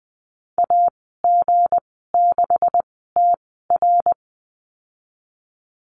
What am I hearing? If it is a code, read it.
Morse code, heard as a keyed tone, about 20 wpm, spaced as for 15 wpm, 700 Hz. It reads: AG6TR
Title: 186 Lesson 30g - Letter b - Call Signs 20wpm